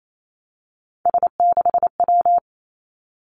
Morse code, heard as a keyed tone, about 28 wpm, 700 Hz. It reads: S6W
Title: Three Challenging Letter-Number Combinations 28wpm